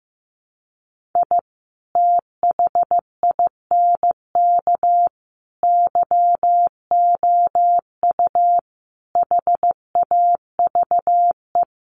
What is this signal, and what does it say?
Morse code, heard as a keyed tone, about 15 wpm, 700 Hz. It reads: I THINK YOU HAVE
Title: Sets of 5 Words 15wpm